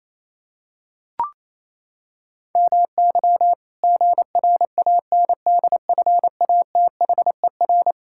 Morse code, heard as a keyed tone, about 28 wpm, 700 Hz. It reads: MY GRANDFATHER
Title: Pangram Challenge v2 - 28wpm